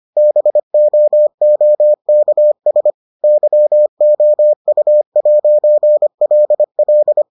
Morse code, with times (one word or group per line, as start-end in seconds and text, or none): 0.2-2.9 BOOKS
3.2-7.2 YOU'LL